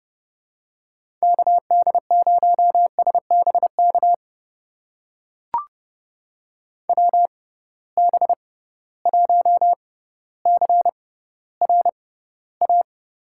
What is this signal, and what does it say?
Morse code, heard as a keyed tone, about 30 wpm, spaced as for 15 wpm, 700 Hz. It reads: KD0SBK WB1CRA
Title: USA Call Signs - 6 Characters - 6x Character Spacing - 30wpm